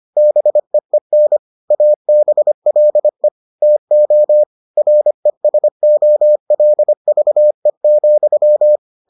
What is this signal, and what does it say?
Morse code, heard as a keyed tone, about 25 wpm, 600 Hz. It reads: BEEN ABLE TO RESOLVE,